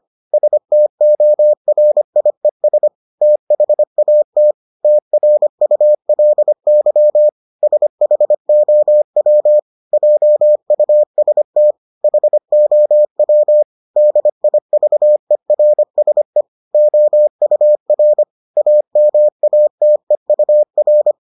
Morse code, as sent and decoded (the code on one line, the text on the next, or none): ... - --- .-. .. ... / - .... .- - / - .-. ..- .-.. -.-- / ... .... --- .-- / .--- ..- ... - / .... --- .-- / -.. .. ...- . .-. ... / --- ..- .-. / .- -- .- - . ..- .-.
STORIES THAT TRULY SHOW JUST HOW DIVERSE OUR AMATEUR